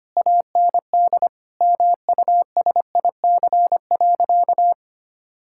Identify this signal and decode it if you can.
Morse code, heard as a keyed tone, about 25 wpm, 700 Hz. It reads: AND MUSIC.
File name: practice / 25wpm / Difficult Sentences with Hints - 25wpm